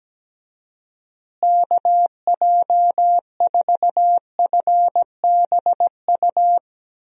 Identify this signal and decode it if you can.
Morse code, heard as a keyed tone, about 17 wpm, 700 Hz. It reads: KJ4FBU
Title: USA Call Signs - 6 Characters - Repeated 2x - 17wpm